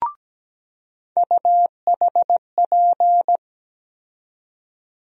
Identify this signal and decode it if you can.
Morse code, heard as a keyed tone, about 17 wpm, 700 Hz. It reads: UHP
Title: USA Call Sign Postfixes - 3 Letters 17wpm